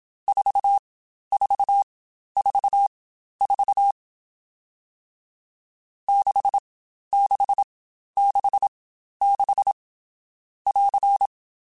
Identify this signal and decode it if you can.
Morse code, heard as a keyed tone, about 27 wpm, spaced as for 12 wpm, 800 Hz. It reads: 4444 6666 <AR>
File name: Short Runs / Short Runs-mp3s / Staccato@25